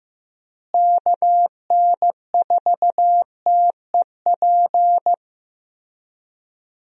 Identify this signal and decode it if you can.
Morse code, heard as a keyed tone, about 16 wpm, 700 Hz. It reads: KN4TEP
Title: USA Call Signs - 6 Characters - Repeated 3x - 15wpm